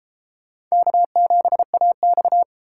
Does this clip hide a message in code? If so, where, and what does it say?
Yes: Morse code from 0.7 to 2.4 s, heard as a keyed tone, about 35 wpm, 700 Hz. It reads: K7AX